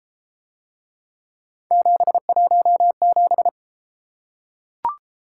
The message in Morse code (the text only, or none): 717